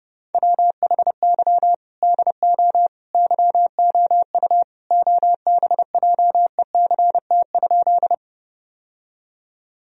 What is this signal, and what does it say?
Morse code, heard as a keyed tone, about 30 wpm, 700 Hz. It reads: WHY DO YOU OBJECT?
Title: Sentences from Top 1000 Words - Review Part 3 - 30wpm